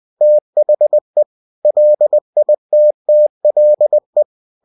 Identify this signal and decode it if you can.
Morse code, heard as a keyed tone, about 20 wpm, 600 Hz. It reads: THE LITTLE
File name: cw-books / winnie-the-pooh / winnie-20wpm-page-0101